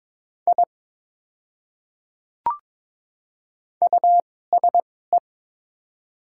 Morse code, heard as a keyed tone, about 22 wpm, spaced as for 16 wpm, 700 Hz. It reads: I USE
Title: 2x Character Spacing - Top 100 Words - 22wpm